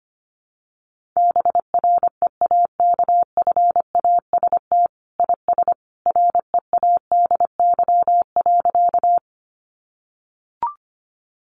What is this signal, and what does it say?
Morse code, heard as a keyed tone, about 25 wpm, 700 Hz. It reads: BREAKFAST IS READY.
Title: Sentences from Top 1500 Words 25wpm